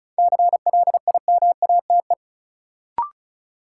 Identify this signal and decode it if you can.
Morse code, heard as a keyed tone, about 36 wpm, 700 Hz. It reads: CLIMATE
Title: Sentences from Top 1600 Words - New Word Drill - 35wpm